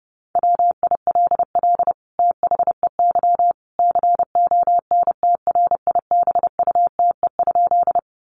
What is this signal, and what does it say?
Morse code, heard as a keyed tone, about 30 wpm, 700 Hz. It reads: WILL THEY CONTRIBUTE?
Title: Sentences from Top 1400 Words 30wpm